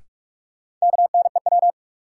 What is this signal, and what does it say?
Morse code, heard as a keyed tone, about 42 wpm, 700 Hz. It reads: KNEW
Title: Sentences from Top 400 Words - New Word Drill - 45wpm